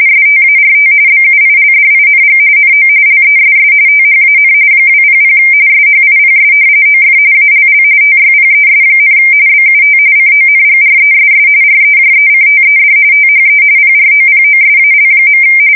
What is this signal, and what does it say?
RTTY, about 45.45 baud. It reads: START BUILDING NO LESS THAN 200 SHIPS OF ALL TYPES, BUT CONGRESS REJECTED THE PROPOSAL.
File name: fsk170 / usnavy3